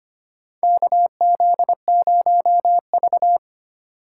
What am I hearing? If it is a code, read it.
Morse code, heard as a keyed tone, about 25 wpm, 700 Hz. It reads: KZ0V
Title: USA Call Signs - 4 Characters - 2x Character Spacing - 25wpm